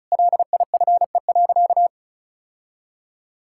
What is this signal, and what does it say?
Morse code, heard as a keyed tone, about 36 wpm, 700 Hz. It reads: LIFE.
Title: Sentences from Top 1000 Words - Review Part 5 - 35wpm